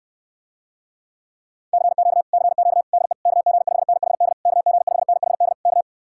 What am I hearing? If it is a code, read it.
Morse code, heard as a keyed tone, about 65 wpm, 700 Hz. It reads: CQ CQ DE KG5NHD KG5NHD K